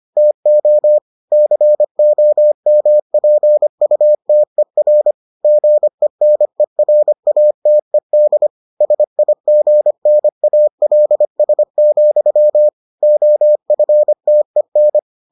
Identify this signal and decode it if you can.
Morse code, heard as a keyed tone, about 25 wpm, 600 Hz. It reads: TO COMPUTER GENERATED SIGNALS, OFTEN